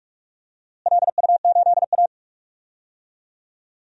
Morse code, heard as a keyed tone, about 46 wpm, 700 Hz. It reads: LU8A
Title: DXCC-100 Call Signs - 4 Characters - Repeated 3x - 45wpm